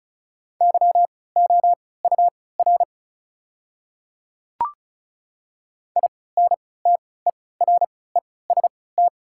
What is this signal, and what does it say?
Morse code, heard as a keyed tone, about 35 wpm, spaced as for 20 wpm, 690 Hz. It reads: YOUR INTEREST
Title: 3x Character Spacing - Top 500 Words - 35wpm